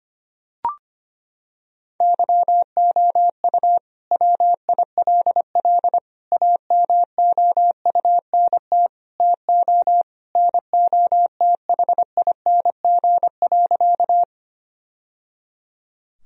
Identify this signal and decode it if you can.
Morse code, heard as a keyed tone, about 25 wpm, 700 Hz. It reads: YOU WILL AMOUNT TO NOTHING.
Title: Sentences from Top 1000 Words - Review Part 4 - 25wpm